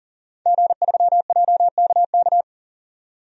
Morse code, heard as a keyed tone, about 40 wpm, 700 Hz. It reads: G3JKK
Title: DXCC-100 Call Signs - 5 Characters - 40wpm